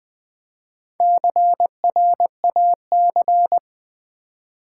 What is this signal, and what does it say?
Morse code, heard as a keyed tone, about 20 wpm, 700 Hz. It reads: CRAC
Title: Build A Word - 1 Letter At A Time - 5-Letter Words - 20wpm